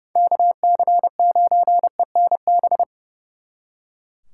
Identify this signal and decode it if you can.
Morse code, heard as a keyed tone, about 37 wpm, 700 Hz. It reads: KC9ENB
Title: Speed Racing - USA 2x3 Call Signs 30wpm